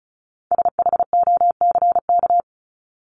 Morse code, heard as a keyed tone, about 34 wpm, 700 Hz. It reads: SHOCK
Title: Top 1751-2000 Words 35wpm